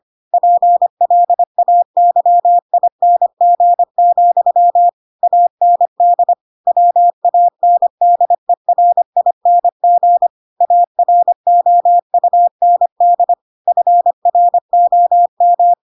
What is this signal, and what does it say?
Morse code, heard as a keyed tone, about 25 wpm, 700 Hz. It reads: PLAYING, AND WANDERING AROUND FROM
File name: Pinocchio / Pinocchio ch4 25 25 ik1hge